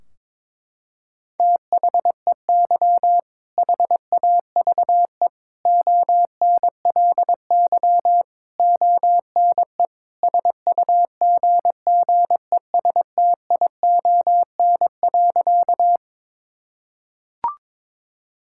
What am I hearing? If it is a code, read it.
Morse code, heard as a keyed tone, about 22 wpm, 700 Hz. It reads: THEY HAVE ONLY ONE SUGGESTION.
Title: Sentences from Top 1600 Words 22wpm